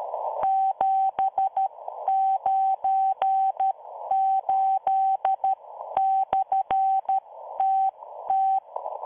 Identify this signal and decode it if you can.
Morse code, heard as a keyed tone, about 13 wpm, spaced as for 11 wpm, 770 Hz. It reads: 798/TT